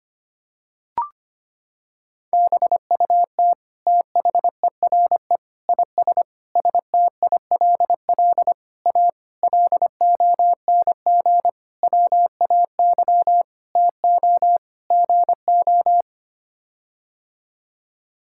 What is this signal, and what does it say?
Morse code, heard as a keyed tone, about 25 wpm, 700 Hz. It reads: BUT THERE IS STILL A LONG WAY TO GO
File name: Sets of 9 Words - 25wpm